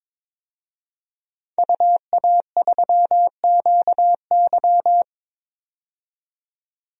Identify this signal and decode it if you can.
Morse code, heard as a keyed tone, about 22 wpm, 700 Hz. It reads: UA3QY
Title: DXCC-100 Call Signs - 5 Characters - 22wpm